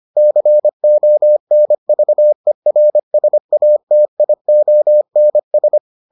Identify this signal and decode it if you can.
Morse code, heard as a keyed tone, about 25 wpm, 600 Hz. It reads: CONVERSATIONS